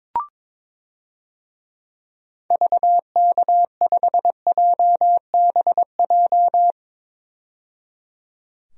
Morse code, heard as a keyed tone, about 22 wpm, 700 Hz. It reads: VK5JBJ
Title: DXCC-100 Call Signs - 6 Characters - 22wpm